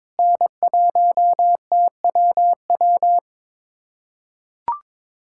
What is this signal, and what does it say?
Morse code, heard as a keyed tone, about 22 wpm, 700 Hz. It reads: N1TWW
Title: USA Callsign DE Callsign - 5 Characters - 22wpm